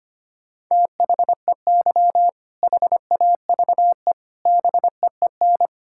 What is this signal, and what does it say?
Morse code, heard as a keyed tone, about 25 wpm, 700 Hz. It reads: THEY HAVE BEEN
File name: practice / 25wpm / Sentences with 3-Word Phrases - 25wpm